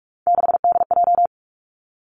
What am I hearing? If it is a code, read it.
Morse code, heard as a keyed tone, about 46 wpm, 700 Hz. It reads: 6DJ
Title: Three Challenging Letter-Number Combinations 45wpm